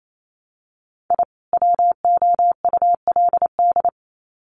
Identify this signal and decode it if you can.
Morse code, heard as a keyed tone, about 28 wpm, 700 Hz. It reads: I WOULD
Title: Sets of 2 Words 28wpm